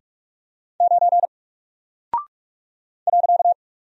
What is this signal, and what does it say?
Morse code, heard as a keyed tone, about 44 wpm, 700 Hz. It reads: 9 .
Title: Single Character (Letter:Number:Punctuation:Prosign) Alt 45wpm